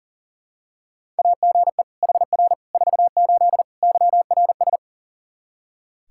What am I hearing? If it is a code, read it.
Morse code, heard as a keyed tone, about 40 wpm, 700 Hz. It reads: AGE HR 48 YRS
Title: Practice QSO - Exchange 3 - More Info - 40wpm